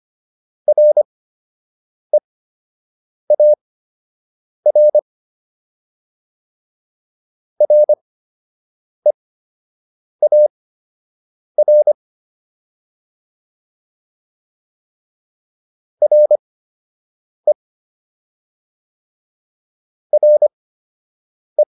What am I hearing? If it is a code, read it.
Morse code, heard as a keyed tone, about 25 wpm, spaced as for 7 wpm, 600 Hz. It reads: REAR REAR RE RE